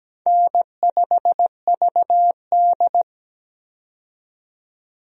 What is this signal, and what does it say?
Morse code, heard as a keyed tone, about 17 wpm, 700 Hz. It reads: N5VD
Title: USA Call Signs - 4 Characters - 17wpm